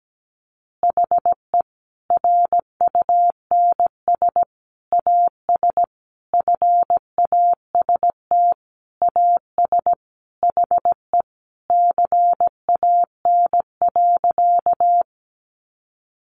Morse code, heard as a keyed tone, about 17 wpm, 700 Hz. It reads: HE RUNS AS FAST AS HE CAN.
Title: Additional Easy Phrases 17wpm